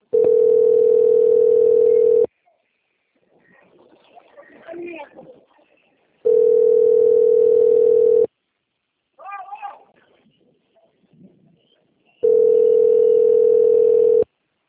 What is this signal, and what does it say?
Ringback tone (North America), 440 + 480 Hz.